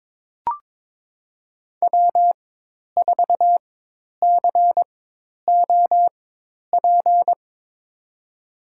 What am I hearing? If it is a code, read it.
Morse code, heard as a keyed tone, about 22 wpm, spaced as for 10 wpm, 700 Hz. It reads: W4COP
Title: USA Call Signs - 5 Characters - 4x Character Spacing - 22wpm